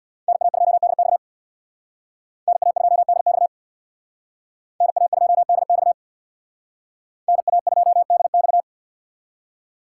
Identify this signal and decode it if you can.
Morse code, heard as a keyed tone, about 59 wpm, 700 Hz. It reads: NA2DX NA2DX NA2DX NA2DX
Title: Speed Racing - USA Call Signs 50wpm